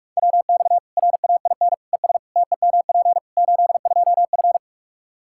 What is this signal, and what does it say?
Morse code, heard as a keyed tone, about 46 wpm, 700 Hz. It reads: WX RAIN ES TEMP 82F
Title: Practice QSO - Exchange 2 - Info - 45wpm